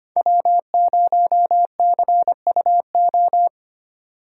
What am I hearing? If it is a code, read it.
Morse code, heard as a keyed tone, about 30 wpm, 700 Hz. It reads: W0CUO
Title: Speed Racing - USA 1x3 Call Signs 25wpm